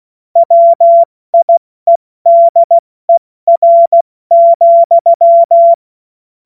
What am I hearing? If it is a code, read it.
Morse code, heard as a keyed tone, about 16 wpm, spaced as for 14 wpm, 680 Hz. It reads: WIEDER,
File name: Morsekurs / Witze / Witz 16. Dezember 2025 70 F80